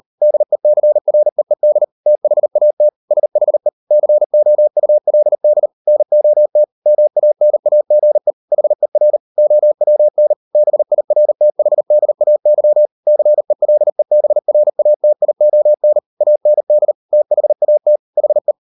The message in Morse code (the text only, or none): DECREED THAT SHE COULD NOT MANAGE HER OWN BIRTHDAY CELEBRATION AND THAT HE